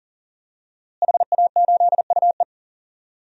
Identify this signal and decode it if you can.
Morse code, heard as a keyed tone, about 40 wpm, 700 Hz. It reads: HA8UE
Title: DXCC-100 Call Signs - 5 Characters - Repeated 2x - 40wpm